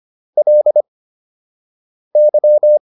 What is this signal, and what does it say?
Morse code, heard as a keyed tone, about 25 wpm, spaced as for 6 wpm, 600 Hz. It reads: LY